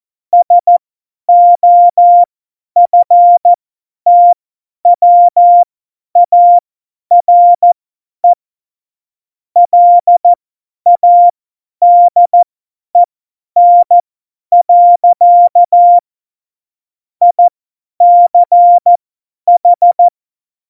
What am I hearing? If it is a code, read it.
Morse code, heard as a keyed tone, about 14 wpm, spaced as for 10 wpm, 700 Hz. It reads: SOFTWARE LADEN. ICH
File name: Morsekurs / Witze / Witz 29. Januar 2019, 50 F70